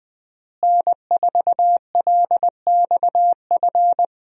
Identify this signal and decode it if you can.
Morse code, heard as a keyed tone, about 20 wpm, 700 Hz. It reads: N4LXF
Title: USA Call Signs - 5 Characters - Repeated 2x - 20wpm